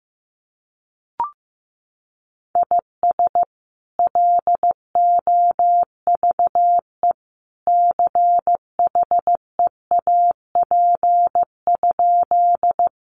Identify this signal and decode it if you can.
Morse code, heard as a keyed tone, about 15 wpm, 700 Hz. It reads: IS LOVE CHEAP?
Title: Sentences from Top 1200 Words 15wpm